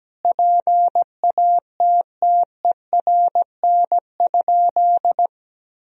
Morse code, heard as a keyed tone, about 17 wpm, 700 Hz. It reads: PATTERN?
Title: Sentences from Top 1000 Words - Review Part 2 - 17wpm